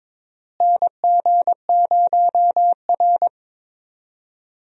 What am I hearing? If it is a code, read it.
Morse code, heard as a keyed tone, about 22 wpm, 700 Hz. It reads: NG0R
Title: USA Call Signs - 4 Characters - Repeated 2x - 22wpm